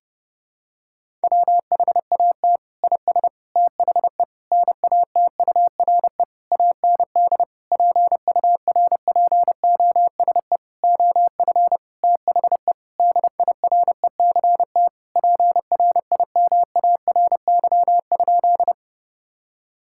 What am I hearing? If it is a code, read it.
Morse code, heard as a keyed tone, about 30 wpm, 700 Hz. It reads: WHAT IS THE NATURE AND PURPOSE OF THE DIRECT PRIMARY?
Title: Sentences from Top 1000 Words - Review Part 5 - 30wpm